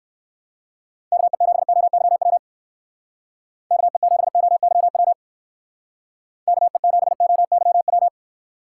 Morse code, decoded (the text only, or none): KE7OYW KE7OYW KE7OYW